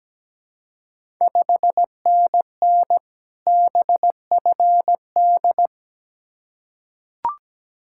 Morse code, heard as a keyed tone, about 17 wpm, 690 Hz. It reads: 5NN BFD
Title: Ontario QSO Party Exchange 17wpm